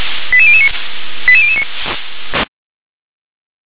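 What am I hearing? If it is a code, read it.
Five-tone selective call (ZVEI), heard as quick runs of five tones, tones of about 70 ms. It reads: 88009 88009
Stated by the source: ZVEI five-tone call 88083 received on SDR.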